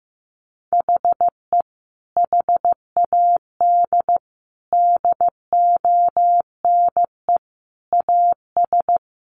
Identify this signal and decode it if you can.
Morse code, heard as a keyed tone, about 15 wpm, 700 Hz. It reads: HE HAD DONE AS
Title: Sentences from Top 1200 Words 15wpm